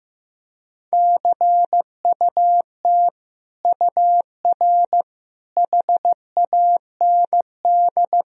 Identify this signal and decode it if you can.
Morse code, heard as a keyed tone, about 15 wpm, 700 Hz. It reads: CUT UR HAND
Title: Sentences from Top 300 Words and Abbreviations 15wpm